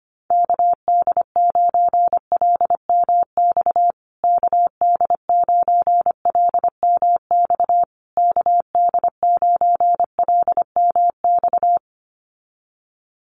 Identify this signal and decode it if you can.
Morse code, heard as a keyed tone, about 25 wpm, 700 Hz. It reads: KD9LMX KD9LMX KD9LMX
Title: USA Call Signs Repeated 3x 25wpm